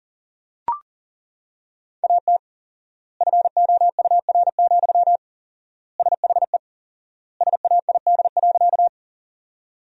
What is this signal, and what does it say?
Morse code, heard as a keyed tone, about 40 wpm, 700 Hz. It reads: AT FOUR, SHE SAID.
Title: 4x Word-Spacing - Sentences from Top 300 Words - 40wpm